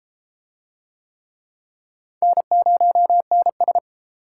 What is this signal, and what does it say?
Morse code, heard as a keyed tone, about 30 wpm, 700 Hz. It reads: N0NS